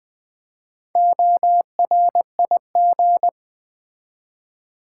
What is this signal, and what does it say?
Morse code, heard as a keyed tone, about 20 wpm, 700 Hz. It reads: ORIG
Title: Build A Word - 4 Letters At A Time - 8-Letter Words - 20wpm